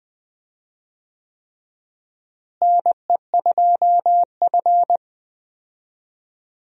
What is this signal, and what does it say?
Morse code, heard as a keyed tone, about 20 wpm, 700 Hz. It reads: NE2F
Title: USA Call Signs - 4 Characters - Repeated 3x - 20wpm